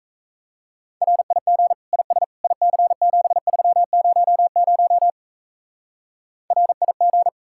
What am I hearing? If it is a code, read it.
Morse code, heard as a keyed tone, about 32 wpm, 700 Hz. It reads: RIG IS IC7300 RIG
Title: Speed-Racing Common QSO Phrases Encore - 28wpm